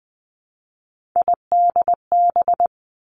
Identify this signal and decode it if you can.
Morse code, heard as a keyed tone, about 19 wpm, 700 Hz. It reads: IDB
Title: Three Challenging Letter-Number Combinations 20wpm